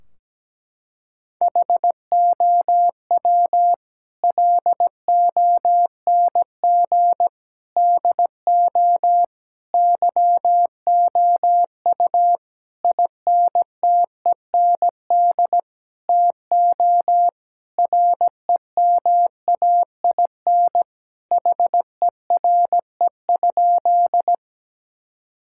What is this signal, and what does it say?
Morse code, heard as a keyed tone, about 17 wpm, 700 Hz. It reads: HOW LONG DO YOU INTEND TO REMAIN HERE?
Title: Sentences from Top 1600 Words 17wpm